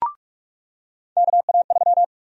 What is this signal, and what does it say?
Morse code, heard as a keyed tone, about 46 wpm, 700 Hz. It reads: KA3
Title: USA Call Sign Prefixes - 45wpm